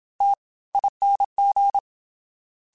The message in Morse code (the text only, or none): TING